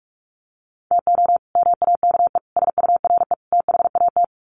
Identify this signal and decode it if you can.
Morse code, heard as a keyed tone, about 44 wpm, 700 Hz. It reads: TO MAKE SURE THAT
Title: Sets of 4 Words 45wpm